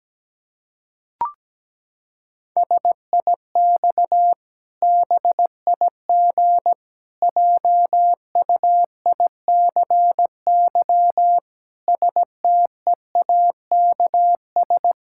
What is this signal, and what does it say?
Morse code, heard as a keyed tone, about 17 wpm, 700 Hz. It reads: SIX BIG JUICY STEAKS
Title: Pangram Challenge v2 - 17wpm